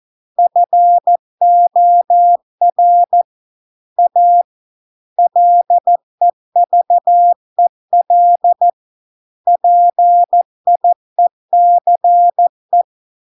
Morse code, heard as a keyed tone, about 14 wpm, 700 Hz. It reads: FOR A LEVEL PIECE